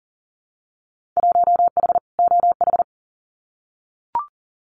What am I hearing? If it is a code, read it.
Morse code, heard as a keyed tone, about 38 wpm, 700 Hz. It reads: 1H OH
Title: Winter Field Day Exchanges 40wpm